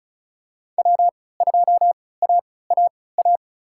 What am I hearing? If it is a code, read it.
Morse code, heard as a keyed tone, about 36 wpm, spaced as for 20 wpm, 700 Hz. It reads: W2AAA
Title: USA Call Signs - 5 Characters - 3x Character Spacing - 35wpm